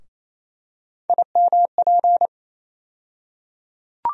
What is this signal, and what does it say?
Morse code, heard as a keyed tone, about 28 wpm, 700 Hz. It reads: IMP
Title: USA Call Sign Postfixes - 3 Letters 28wpm